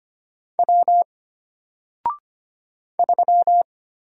Morse code, heard as a keyed tone, about 25 wpm, 700 Hz. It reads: W 3
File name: Single-Letter-Number-25wpm